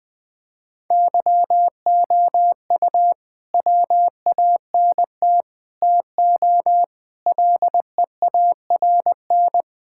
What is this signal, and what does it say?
Morse code, heard as a keyed tone, about 20 wpm, 700 Hz. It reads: YOU WANT TO LEARN
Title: Sets of 6 Words - 20wpm